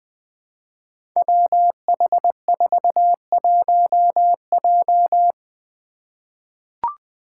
Morse code, heard as a keyed tone, about 20 wpm, 700 Hz. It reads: WH41J
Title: Five Challenging Letter-Number Combinations 20wpm